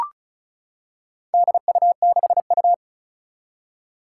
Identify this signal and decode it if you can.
Morse code, heard as a keyed tone, about 36 wpm, 700 Hz. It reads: DU6U
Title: Four Challenging Letter-Number Combinations 35wpm